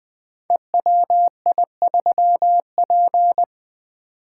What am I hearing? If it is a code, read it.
Morse code, heard as a keyed tone, about 20 wpm, 700 Hz. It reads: EWI3P
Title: Code Groups - 5 Letter:Numbers - 20wpm